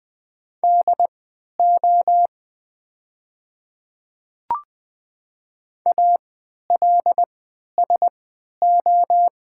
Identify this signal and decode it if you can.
Morse code, heard as a keyed tone, about 20 wpm, spaced as for 11 wpm, 700 Hz. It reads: DO ALSO
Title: 3x Character Spacing - Top 200 Words - 20wpm